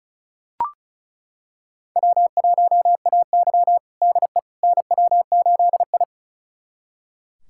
Morse code, heard as a keyed tone, about 36 wpm, 700 Hz. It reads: W1AY DE NW8I
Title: USA Callsign DE Callsign - 4 Characters - 35wpm